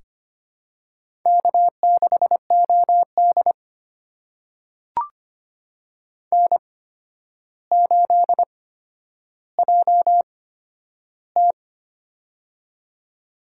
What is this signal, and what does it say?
Morse code, heard as a keyed tone, about 25 wpm, spaced as for 11 wpm, 700 Hz. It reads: K6OD N8JT
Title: USA Call Signs - 4 Characters - 8x Character Spacing - 25wpm